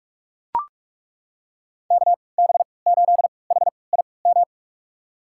Morse code, heard as a keyed tone, about 41 wpm, spaced as for 25 wpm, 700 Hz. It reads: KB8HIM
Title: USA Call Signs - 6 Characters - 3x Character Spacing - 45wpm